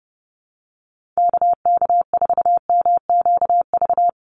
Morse code, heard as a keyed tone, about 30 wpm, 700 Hz. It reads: KK4MQV
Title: USA Call Signs 30wpm